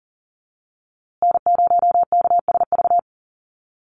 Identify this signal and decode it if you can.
Morse code, heard as a keyed tone, about 40 wpm, 700 Hz. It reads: N0KSV
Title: USA Call Signs Repeated 2x 40wpm